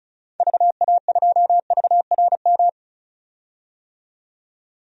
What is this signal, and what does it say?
Morse code, heard as a keyed tone, about 34 wpm, 700 Hz. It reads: VA2VRM
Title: DXCC-100 Call Signs - 6 Characters - 35wpm